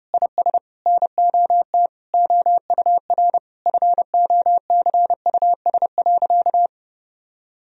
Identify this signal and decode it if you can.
Morse code, heard as a keyed tone, about 30 wpm, 700 Hz. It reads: IS NOT OUR FOCUS.
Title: Sentences from Top 1000 Words - Review Part 4 - 30wpm